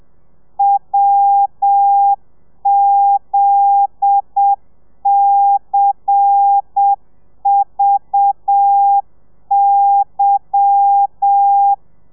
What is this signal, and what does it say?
Morse code, heard as a keyed tone, about 7 wpm, 800 Hz. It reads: WZCVY
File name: telegrafia / i2tfj / Lez10 30